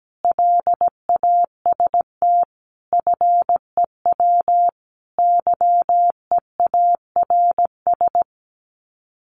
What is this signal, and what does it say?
Morse code, heard as a keyed tone, about 17 wpm, 700 Hz. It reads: LAST FEW YEARS
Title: Sets of 5 Words Encore 17wpm